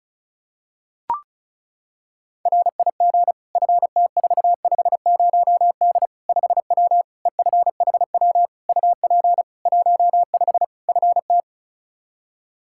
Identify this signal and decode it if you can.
Morse code, heard as a keyed tone, about 36 wpm, 700 Hz. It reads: RIG FT450D 5W EFHW UP 15 FT
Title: Practice QSO - Exchange 2 - Info - 35wpm